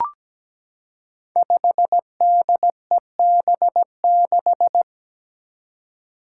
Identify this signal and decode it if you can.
Morse code, heard as a keyed tone, about 17 wpm, 700 Hz. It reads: 5DEB6
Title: Five Challenging Letter-Number Combinations 17wpm